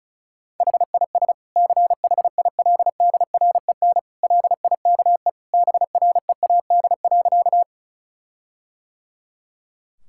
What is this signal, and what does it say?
Morse code, heard as a keyed tone, about 34 wpm, 700 Hz. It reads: HIS CHILDREN LIKE BREAD.
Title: Sentences from Top 1600 Words 35wpm